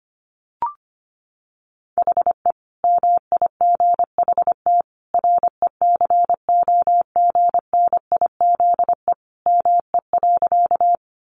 Morse code, heard as a keyed tone, about 25 wpm, 700 Hz. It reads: HE MIGHT RECOGNIZE ME.
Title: Sentences from Top 1400 Words 25wpm